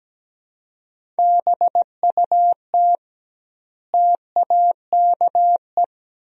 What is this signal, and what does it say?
Morse code, heard as a keyed tone, about 17 wpm, 700 Hz. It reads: BUT TAKE
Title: 2x Word-Spacing - Sentences from Top 300 Words - 17wpm